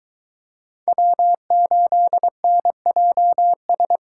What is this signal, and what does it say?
Morse code, heard as a keyed tone, about 22 wpm, 700 Hz. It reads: W8NJS